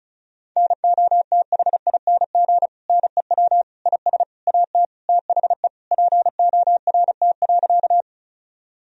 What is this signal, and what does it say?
Morse code, heard as a keyed tone, about 34 wpm, 700 Hz. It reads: NOTHING NEW IS AT THE PORT.
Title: Sentences from Top 1000 Words - Review Part 1 - 35wpm